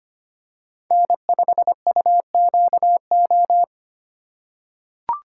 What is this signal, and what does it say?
Morse code, heard as a keyed tone, about 25 wpm, 700 Hz. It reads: N5UQO
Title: USA Call Signs - 5 Characters - 3x Character Spacing - 25wpm